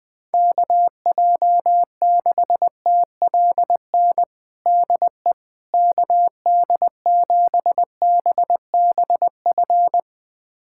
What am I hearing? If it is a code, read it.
Morse code, heard as a keyed tone, about 21 wpm, 700 Hz. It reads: KJ6TLN DE KD7BBF
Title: USA Callsign DE Callsign - 6 Characters - 20wpm